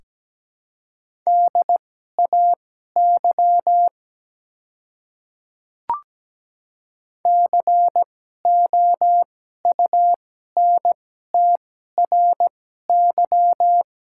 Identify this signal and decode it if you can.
Morse code, heard as a keyed tone, about 17 wpm, spaced as for 12 wpm, 700 Hz. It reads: DAY COUNTRY
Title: 2x Character Spacing - Top 300 Words - 17wpm